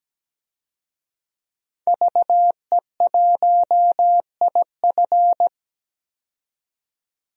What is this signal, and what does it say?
Morse code, heard as a keyed tone, about 17 wpm, 700 Hz. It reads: VE1IF
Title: DXCC-100 Call Signs - 5 Characters - 17wpm